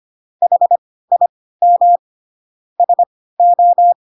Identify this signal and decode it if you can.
Morse code, heard as a keyed tone, about 25 wpm, spaced as for 16 wpm, 700 Hz. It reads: HIM SO
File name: Pinocchio / Pinocchio ch2 10 25 ik1hge